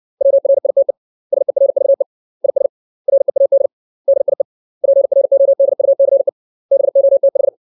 Morse code, heard as a keyed tone, about 60 wpm, 550 Hz. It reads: WRITE SERVE IS READ DIE PRODUCE BOTH
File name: 632 words / 632 Words 60WPM